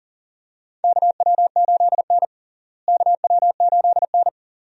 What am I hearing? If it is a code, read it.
Morse code, heard as a keyed tone, about 40 wpm, 700 Hz. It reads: KW8N KW8N
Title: USA Call Signs - 4 Characters - Repeated 2x - 40wpm